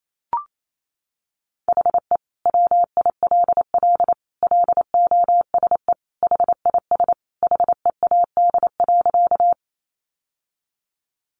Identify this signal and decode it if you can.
Morse code, heard as a keyed tone, about 28 wpm, 700 Hz. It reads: HE WILL LOSE HIS HEAD.
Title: Sentences from Top 1100 Words 28wpm